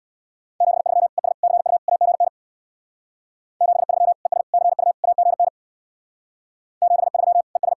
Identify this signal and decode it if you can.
Morse code, heard as a keyed tone, about 67 wpm, 700 Hz. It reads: 73 ES CU AGN 73 ES CU AGN 73 ES